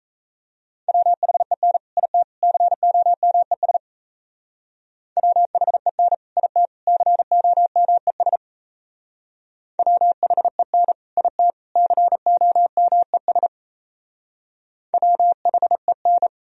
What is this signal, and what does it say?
Morse code, heard as a keyed tone, about 36 wpm, 700 Hz. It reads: WHEN IT COMES WHEN IT COMES WHEN IT COMES WHEN